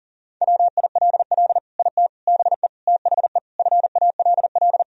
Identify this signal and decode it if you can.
Morse code, heard as a keyed tone, about 40 wpm, 700 Hz. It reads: WILL IT BE THE FALL
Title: Sentences from Top 1000 Words - Review Part 2 - 40wpm